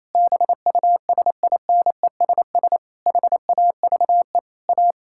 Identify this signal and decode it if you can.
Morse code, heard as a keyed tone, about 28 wpm, 700 Hz. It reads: BUSINESS HAVE A
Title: Sentences from Top 1600 Words 28wpm